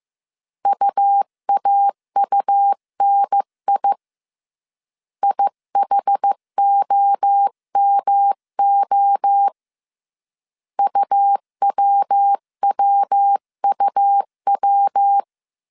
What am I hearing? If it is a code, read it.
Morse code, heard as a keyed tone, about 15 wpm, 800 Hz. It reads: UAUNI IHOMO UWWUW
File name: CW / DET / 1 3 1500C S A 14wpm 2447